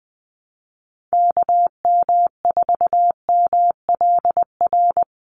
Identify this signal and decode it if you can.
Morse code, heard as a keyed tone, about 20 wpm, 700 Hz. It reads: KM4MLR
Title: USA Call Signs 20wpm